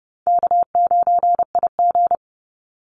Morse code, heard as a keyed tone, about 30 wpm, 700 Hz. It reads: K9IG
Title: Easy Call Signs - 30wpm